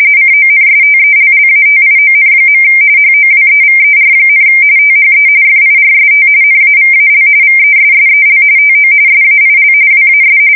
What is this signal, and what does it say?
RTTY, about 45.45 baud. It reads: CONSTELLATION. HOWEVER, THE FIRST NAVAL VESSEL TO SAIL WAS
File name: fsk170 / usnavy1